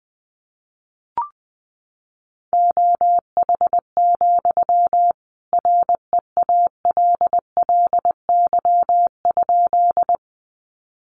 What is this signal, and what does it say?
Morse code, heard as a keyed tone, about 20 wpm, 700 Hz. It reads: OH, REALLY?
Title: Sentences from Top 600 Words 20wpm